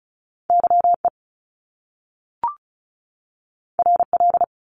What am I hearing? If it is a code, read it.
Morse code, heard as a keyed tone, about 36 wpm, 700 Hz. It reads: YE RL
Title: Top Two Letter Combinations in Words 35wpm